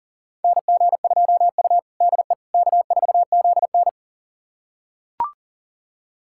Morse code, heard as a keyed tone, about 40 wpm, 700 Hz. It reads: NG2U DE K4ZN
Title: USA Callsign DE Callsign - 4 Characters - 40wpm